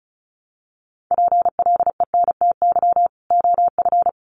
Morse code, heard as a keyed tone, about 34 wpm, 700 Hz. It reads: PLENTY OF
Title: Sentences from Top 1300 Words 35wpm